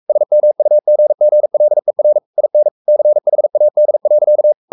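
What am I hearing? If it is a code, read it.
Morse code, heard as a keyed tone, about 44 wpm, 600 Hz. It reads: SMUGGLER IN CHAD.
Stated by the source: Morse code at 43 wpm sending Ken Follett's novel Never in English, from CW med Gnister.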